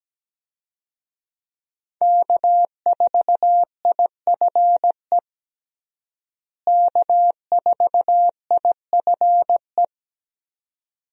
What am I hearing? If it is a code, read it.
Morse code, heard as a keyed tone, about 17 wpm, 700 Hz. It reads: K4IFE K4IFE
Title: USA Call Signs - 5 Characters - Repeated 2x - 17wpm